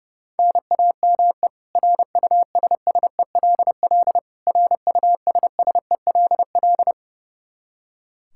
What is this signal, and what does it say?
Morse code, heard as a keyed tone, about 32 wpm, 700 Hz. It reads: NAME RUSSELL RUSSELL
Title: K1USN SST QSO Practice 30wpm